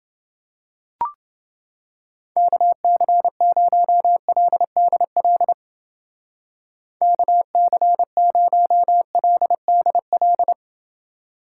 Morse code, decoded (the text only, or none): KC0LDL KC0LDL